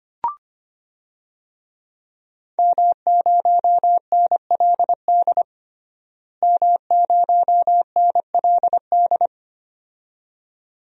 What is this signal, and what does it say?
Morse code, heard as a keyed tone, about 25 wpm, 700 Hz. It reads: M0NLD M0NLD
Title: DXCC-100 Call Signs - 5 Characters - Repeated 2x - 25wpm